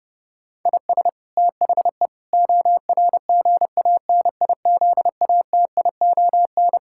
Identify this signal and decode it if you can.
Morse code, heard as a keyed tone, about 32 wpm, 700 Hz. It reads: IS THE ORGANIZATION
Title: Sentences from Top 1000 Words - Review Part 5 - 30wpm